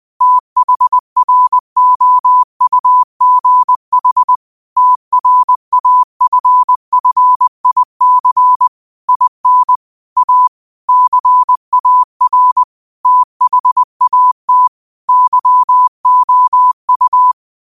Morse code, heard as a keyed tone, about 20 wpm, 1000 Hz. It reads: THROUGH TRAFFIC IN A CAR THAT YOU
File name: 20wpm